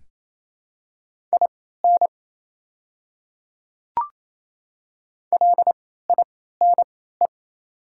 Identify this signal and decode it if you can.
Morse code, heard as a keyed tone, about 27 wpm, spaced as for 16 wpm, 690 Hz. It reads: IN LINE
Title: 3x Character Spacing - Top 200 Words - 28wpm